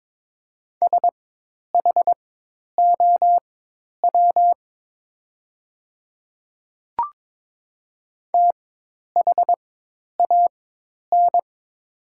Morse code, heard as a keyed tone, about 22 wpm, spaced as for 10 wpm, 700 Hz. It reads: SHOW THAN
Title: 4x Character Spacing - Top 300 Words - 22wpm